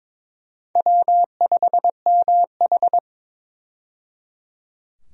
Morse code, heard as a keyed tone, about 22 wpm, 700 Hz. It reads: W5MH
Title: USA Call Signs - 4 Characters - 22wpm